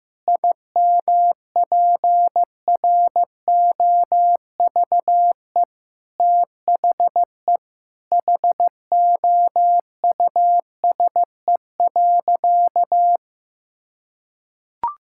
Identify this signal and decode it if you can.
Morse code, heard as a keyed tone, about 15 wpm, 700 Hz. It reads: IMPROVE THE HOUSE.
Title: Sentences from Top 1000 Words - Review Part 5 - 15wpm